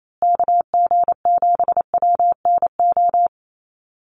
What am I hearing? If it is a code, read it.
Morse code, heard as a keyed tone, about 28 wpm, 700 Hz. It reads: KG7WNO
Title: USA Call Signs Repeated 2x 28wpm